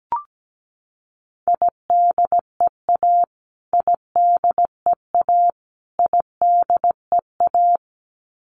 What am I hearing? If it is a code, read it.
Morse code, heard as a keyed tone, about 17 wpm, 700 Hz. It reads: IDEA IDEA IDEA
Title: Top 500 Words Repeated 17wpm